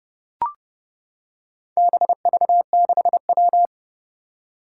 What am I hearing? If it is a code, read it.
Morse code, heard as a keyed tone, about 30 wpm, 700 Hz. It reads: BV6W
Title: Four Challenging Letter-Number Combinations 30wpm